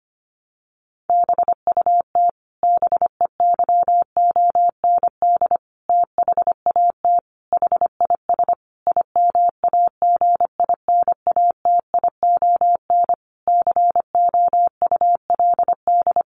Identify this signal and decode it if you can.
Morse code, heard as a keyed tone, about 25 wpm, 700 Hz. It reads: BUT BEYOND THAT HIS IMAGINATION COULD
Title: Sentences from Top 1300 Words 25wpm